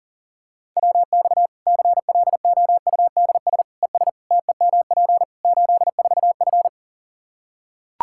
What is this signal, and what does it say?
Morse code, heard as a keyed tone, about 40 wpm, 700 Hz. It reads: WX CLOUDS ES TEMP 84F
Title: Practice QSO - Exchange 2 - Info - 40wpm